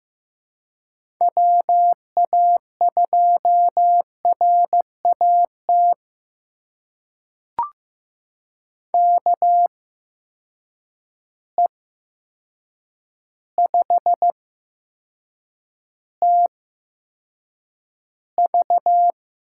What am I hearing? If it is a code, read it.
Morse code, heard as a keyed tone, about 15 wpm, spaced as for 7 wpm, 700 Hz. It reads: WA2RAT KE5TV
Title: USA Call Signs - 6 Characters - 8x Character Spacing - 15wpm